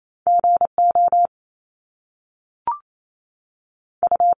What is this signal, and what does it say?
Morse code, heard as a keyed tone, about 28 wpm, 700 Hz. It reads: G O U
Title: Top Two Letter Combinations in Words 28wpm